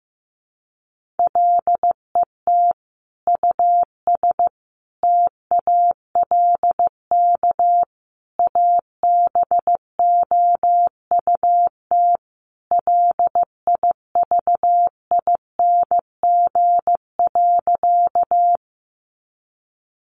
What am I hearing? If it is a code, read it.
Morse code, heard as a keyed tone, about 15 wpm, 700 Hz. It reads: LET US TALK ABOUT LIVING.
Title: Sentences from Top 1100 Words 15wpm